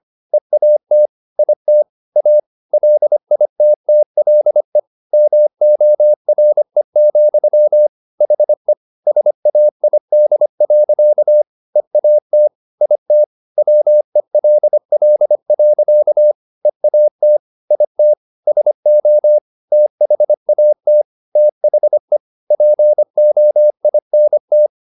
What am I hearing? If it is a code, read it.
Morse code, heard as a keyed tone, about 25 wpm, 600 Hz. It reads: EAT IT A LITTLE MORE, HE SAID. EAT IT WELL. EAT IT SO THAT THE POINT
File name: the old man and the sea part 1 25wpm